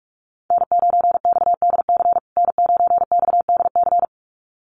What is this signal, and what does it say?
Morse code, heard as a keyed tone, about 44 wpm, 700 Hz. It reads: N9XDC N9XDC
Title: USA Call Signs Repeated 2x 45wpm